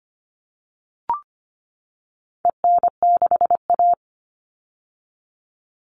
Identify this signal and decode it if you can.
Morse code, heard as a keyed tone, about 25 wpm, 700 Hz. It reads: EN6A
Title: All Cut Numbers - 4-Digit 25wpm